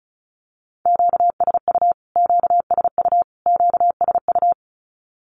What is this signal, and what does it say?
Morse code, heard as a keyed tone, about 34 wpm, 700 Hz. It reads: QSU QSU QSU
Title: Q-Codes Repeated 3x 35wpm